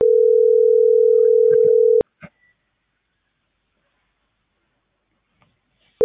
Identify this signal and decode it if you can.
Ringback tone (North America), 440 + 480 Hz.